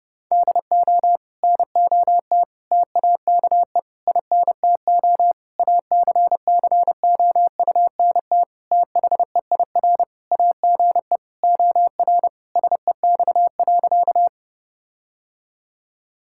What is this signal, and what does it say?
Morse code, heard as a keyed tone, about 30 wpm, 700 Hz. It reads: DO NOT TAKE INTO ACCOUNT THEIR AGE OR SEX.
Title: Sentences from Top 1600 Words 30wpm